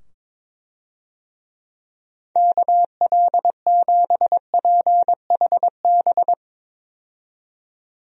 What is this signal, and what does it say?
Morse code, heard as a keyed tone, about 22 wpm, 700 Hz. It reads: KL7PHB
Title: DXCC-100 Call Signs - 6 Characters - 22wpm